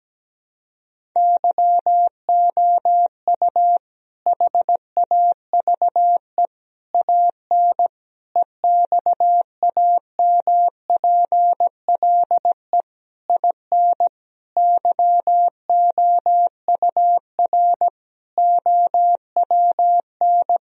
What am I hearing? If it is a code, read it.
Morse code, heard as a keyed tone, about 17 wpm, 700 Hz. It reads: YOU HAVE AN EXAMPLE IN YOUR OWN
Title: Sentences from Top 1000 Words - Review Part 1 - 17wpm